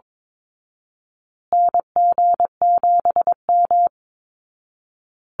Morse code, heard as a keyed tone, about 22 wpm, 700 Hz. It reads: NG7M
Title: Call Signs Repeated 2x - 22wpm